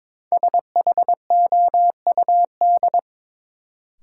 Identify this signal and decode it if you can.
Morse code, heard as a keyed tone, about 21 wpm, 700 Hz. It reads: SHOUD
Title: Sentences from Top 1000 Words - Review Part 4 - New Word Drill - 22wpm